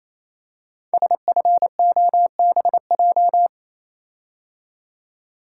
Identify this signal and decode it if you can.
Morse code, heard as a keyed tone, about 29 wpm, 700 Hz. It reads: SFOBJ
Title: Code Groups - 5 Letters - 28wpm